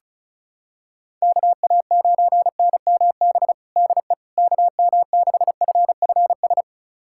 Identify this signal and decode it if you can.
Morse code, heard as a keyed tone, about 34 wpm, 700 Hz. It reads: KA9NMB DE KM6FFS
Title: USA Callsign DE Callsign 35wpm